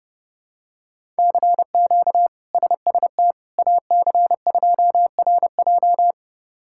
Morse code, heard as a keyed tone, about 30 wpm, 700 Hz. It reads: CQ SST AC2RJ
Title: K1USN SST QSO Practice - 30wpm